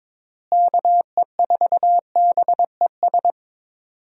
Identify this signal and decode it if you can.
Morse code, heard as a keyed tone, about 21 wpm, 700 Hz. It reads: KE4BES
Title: USA Call Signs - 6 Characters - Repeated 2x - 22wpm